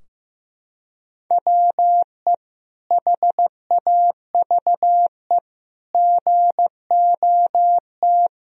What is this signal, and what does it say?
Morse code, heard as a keyed tone, about 15 wpm, 700 Hz. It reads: WE HAVE GOT
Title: Sentences from Top 1000 Words - Review Part 3 - 15wpm